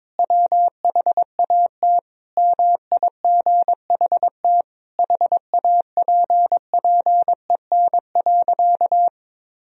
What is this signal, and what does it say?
Morse code, heard as a keyed tone, about 22 wpm, 700 Hz. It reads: WHAT MIGHT HAPPEN.
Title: Sentences from Top 400 Words and Abbreviations 22wpm